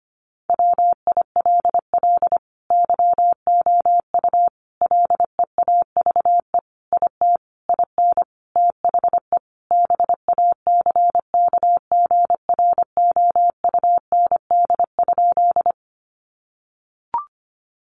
Morse code, heard as a keyed tone, about 25 wpm, 700 Hz. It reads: WILL YOU LEAVE IT IN THE BACKGROUND?
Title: Sentences from Top 1300 Words 25wpm